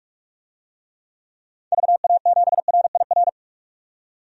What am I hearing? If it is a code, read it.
Morse code, heard as a keyed tone, about 46 wpm, 700 Hz. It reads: VA7RIR
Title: DXCC-100 Call Signs - 6 Characters - 45wpm